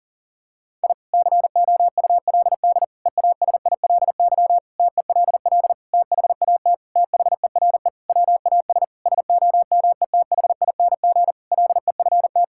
Morse code, heard as a keyed tone, about 37 wpm, 700 Hz. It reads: I COULD EASILY TELL THAT THERE WAS SOMETHING LEFT
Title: Sentences from Top 1000 Words - Review Part 6 - 40wpm